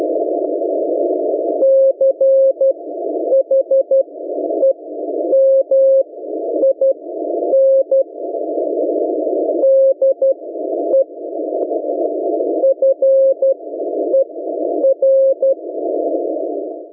Morse code, heard as a keyed tone, about 12 wpm, spaced as for 9 wpm, 540 Hz. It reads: CHEMIN DE FER